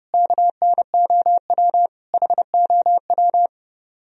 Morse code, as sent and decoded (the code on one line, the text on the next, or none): -.- -. --- .-- / .... --- .--
KNOW HOW